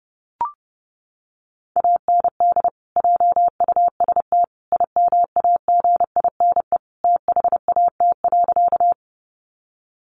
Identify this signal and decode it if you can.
Morse code, heard as a keyed tone, about 30 wpm, 700 Hz. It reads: AND JUST IMAGINE THAT.
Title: Sentences from Top 1400 Words 30wpm